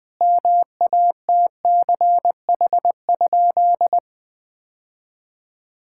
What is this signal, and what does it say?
Morse code, heard as a keyed tone, about 21 wpm, 700 Hz. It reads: MATCH?
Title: Sentences from Top 1600 Words 20wpm